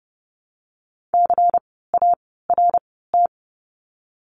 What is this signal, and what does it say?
Morse code, heard as a keyed tone, about 30 wpm, 700 Hz. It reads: C A R T
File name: Lesson 16f - Letter c - Words 30wpm